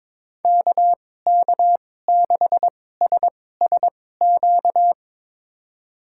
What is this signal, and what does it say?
Morse code, heard as a keyed tone, about 21 wpm, spaced as for 16 wpm, 700 Hz. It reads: KK6SSQ
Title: USA Call Signs - 6 Characters - 2x Character Spacing - 22wpm